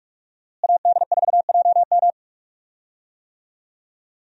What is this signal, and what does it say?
Morse code, heard as a keyed tone, about 42 wpm, 700 Hz. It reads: AD4JM
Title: USA Call Signs - 5 Characters - Repeated 3x - 45wpm